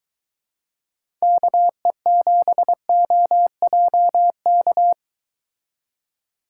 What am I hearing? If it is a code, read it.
Morse code, heard as a keyed tone, about 23 wpm, 700 Hz. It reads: KE7OJK